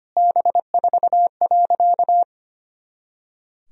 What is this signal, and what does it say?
Morse code, heard as a keyed tone, about 25 wpm, 700 Hz. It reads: B4.
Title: Sentences from Top 400 Words and Abbreviations 25wpm